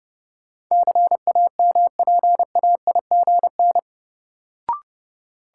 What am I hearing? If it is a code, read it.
Morse code, heard as a keyed tone, about 30 wpm, 700 Hz. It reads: CAMPAIGN
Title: Sentences from Top 1200 Words - New Word Drill Alt - 30wpm